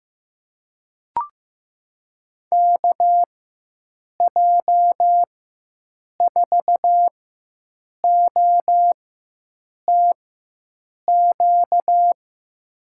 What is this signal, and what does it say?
Morse code, heard as a keyed tone, about 15 wpm, spaced as for 7 wpm, 700 Hz. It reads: KJ4OTQ
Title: USA Call Signs - 6 Characters - 4x Character Spacing - 15wpm